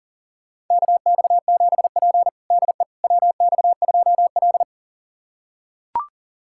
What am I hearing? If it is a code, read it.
Morse code, heard as a keyed tone, about 40 wpm, 700 Hz. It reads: KX7P DE WX2L
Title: USA Callsign DE Callsign - 4 Characters - 40wpm